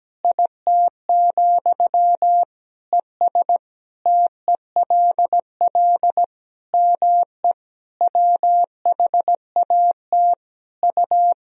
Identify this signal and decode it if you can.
Morse code, heard as a keyed tone, about 17 wpm, 700 Hz. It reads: IT, ES TELL ME WHAT U
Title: Sentences from Top 200 Words and Abbreviations 17wpm